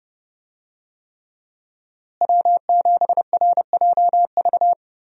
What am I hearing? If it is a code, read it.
Morse code, heard as a keyed tone, about 30 wpm, 700 Hz. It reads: W7RJV
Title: USA Call Signs - 5 Characters - Repeated 3x - 30wpm